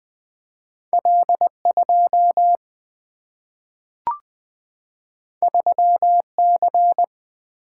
Morse code, heard as a keyed tone, about 20 wpm, 700 Hz. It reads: L 2 3 C
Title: Single-Letter Single-Number 20wpm